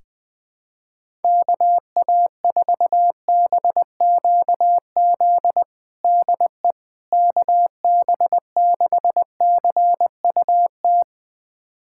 Morse code, heard as a keyed tone, about 21 wpm, 700 Hz. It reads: KA4BQZ DE KB6CUT
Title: USA Callsign DE Callsign - 6 Characters - 20wpm